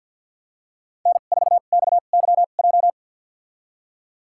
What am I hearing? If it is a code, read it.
Morse code, heard as a keyed tone, about 53 wpm, spaced as for 37 wpm, 700 Hz. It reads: N4XYJ
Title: USA Call Signs - 5 Characters - 2x Character Spacing - 50wpm